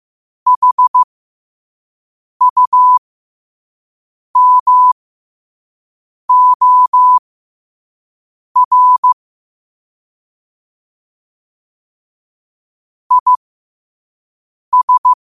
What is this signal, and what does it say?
Morse code, heard as a keyed tone, about 15 wpm, spaced as for 5 wpm, 1000 Hz. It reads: HUMOR IS
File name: qotdfiles / 5wpm